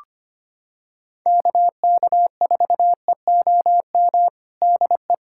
Morse code, heard as a keyed tone, about 25 wpm, 700 Hz. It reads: KK4EOM DE
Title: USA Callsign DE Callsign - 6 Characters - 25wpm